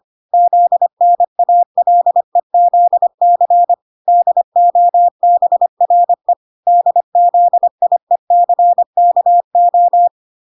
Morse code, heard as a keyed tone, about 25 wpm, 700 Hz. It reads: ZNALEZC DOBRE DZIECKO
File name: 25wpm